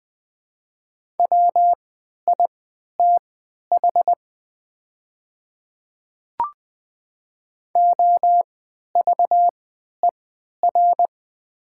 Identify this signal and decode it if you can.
Morse code, heard as a keyed tone, about 20 wpm, spaced as for 11 wpm, 700 Hz. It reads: WITH OVER
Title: 3x Character Spacing - Top 100 Words - 20wpm